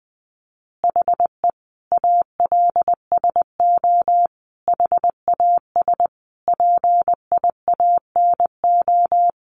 Morse code, heard as a keyed tone, about 20 wpm, 700 Hz. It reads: HE ALSO HAS PIANO
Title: Additional Easy Phrases 20wpm